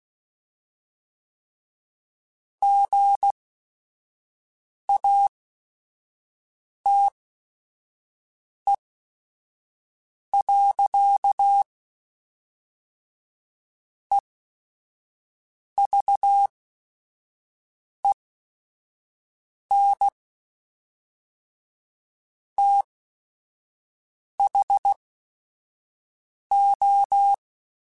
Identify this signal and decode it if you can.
Morse code, heard as a keyed tone, about 16 wpm, spaced as for 5 wpm, 800 Hz. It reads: GATE. EVEN THO
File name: Technical Subjects / Technical Subjects-mp3s / AND Gate @5